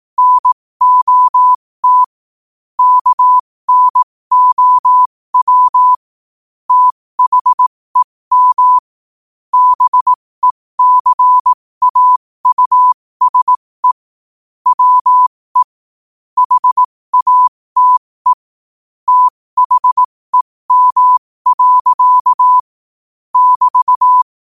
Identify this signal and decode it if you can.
Morse code, heard as a keyed tone, about 18 wpm, spaced as for 15 wpm, 1000 Hz. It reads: NOT KNOW THEM BECAUSE WE HATE THEM. <BT>
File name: qotdfiles / 15wpm